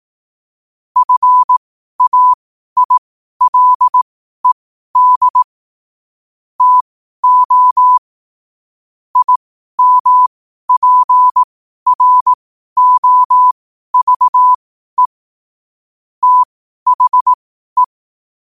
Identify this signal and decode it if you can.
Morse code, heard as a keyed tone, about 18 wpm, spaced as for 12 wpm, 1000 Hz. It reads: FAILED TO IMPROVE THE
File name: qotdfiles / 12wpm